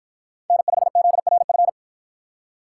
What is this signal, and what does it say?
Morse code, heard as a keyed tone, about 53 wpm, 700 Hz. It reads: N5ZRF